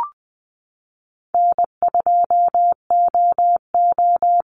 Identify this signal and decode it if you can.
Morse code, heard as a keyed tone, about 20 wpm, 700 Hz. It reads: N2OO
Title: Call Signs - 20wpm